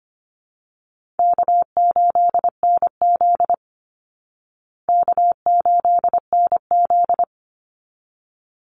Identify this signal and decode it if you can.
Morse code, heard as a keyed tone, about 25 wpm, 700 Hz. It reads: K8NZ K8NZ
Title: Easy Call Signs Repeated 2x - 25wpm